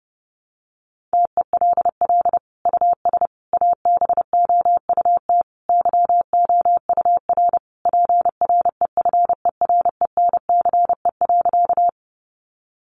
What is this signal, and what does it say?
Morse code, heard as a keyed tone, about 30 wpm, 700 Hz. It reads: TELL US ABOUT YOUR PREFERENCE.
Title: Sentences from Top 1500 Words 30wpm